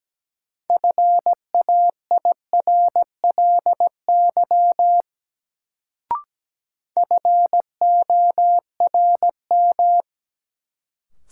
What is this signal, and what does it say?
Morse code, heard as a keyed tone, about 17 wpm, 700 Hz. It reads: FAIRLY FORM
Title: Sentences from Top 1100 Words - New Word Drill Alt - 17wpm